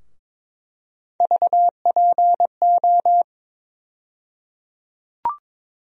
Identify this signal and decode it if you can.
Morse code, heard as a keyed tone, about 22 wpm, 700 Hz. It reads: VPO
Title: USA Call Sign Postfixes - 3 Letters 22wpm